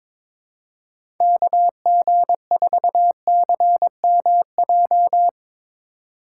Morse code, heard as a keyed tone, about 22 wpm, 700 Hz. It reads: KG4CMJ
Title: USA Call Signs - 6 Characters - 2x Character Spacing - 22wpm